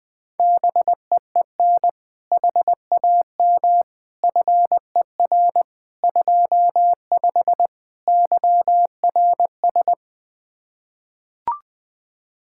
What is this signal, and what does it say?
Morse code, heard as a keyed tone, about 20 wpm, 700 Hz. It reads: BEEN HAM FER 25 YRS
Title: Practice QSO - Exchange 3 - More Info - 20wpm